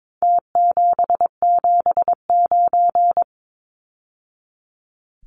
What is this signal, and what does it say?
Morse code, heard as a keyed tone, about 22 wpm, 700 Hz. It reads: T779
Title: Numbers - Contest Style - 4-Digit 22wpm